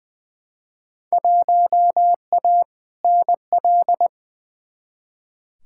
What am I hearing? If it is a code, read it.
Morse code, heard as a keyed tone, about 20 wpm, 700 Hz. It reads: 1A NL
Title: Field Day Exchanges Encore - 20wpm